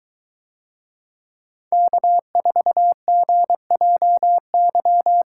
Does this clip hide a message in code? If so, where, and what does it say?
Yes: Morse code from 1.7 to 5.2 s, heard as a keyed tone, about 21 wpm, 700 Hz. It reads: K4GJY